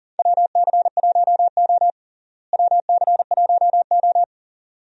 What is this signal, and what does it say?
Morse code, heard as a keyed tone, about 37 wpm, 700 Hz. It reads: WC1O WC1O
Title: USA Call Signs - 4 Characters - Repeated 2x - 40wpm